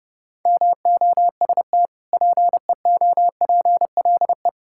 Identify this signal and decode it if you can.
Morse code, heard as a keyed tone, about 30 wpm, 700 Hz. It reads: MOST PEOPLE
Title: Sentences from Top 1000 Words - Review Part 5 - 30wpm